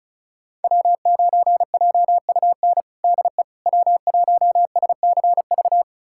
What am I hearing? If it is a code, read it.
Morse code, heard as a keyed tone, about 36 wpm, 700 Hz. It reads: W9JUN DE W1SCV
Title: USA Callsign DE Callsign - 5 Characters - 35wpm